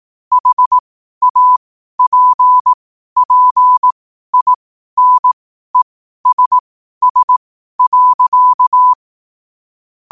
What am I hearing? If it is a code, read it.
Morse code, heard as a keyed tone, about 18 wpm, spaced as for 12 wpm, 1000 Hz. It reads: HAPPINESS.
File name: qotdfiles / 12wpm